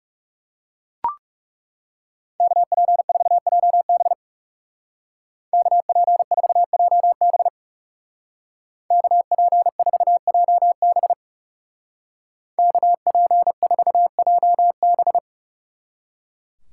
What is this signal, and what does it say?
Morse code, heard as a keyed tone, about 37 wpm, 700 Hz. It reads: KP4JB KP4JB KP4JB KP4JB
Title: Speed Racing - USA 2x2 Call Signs 30wpm